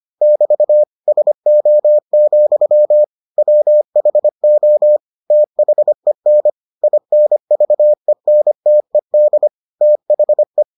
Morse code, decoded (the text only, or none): <BT> SO, WHO THEN INVENTED THE